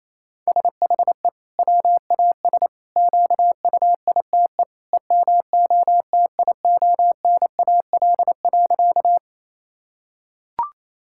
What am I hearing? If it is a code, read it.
Morse code, heard as a keyed tone, about 28 wpm, 700 Hz. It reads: SHE WAS QUITE EMOTIONAL.
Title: Sentences from Top 1000 Words - Review Part 5 - 28wpm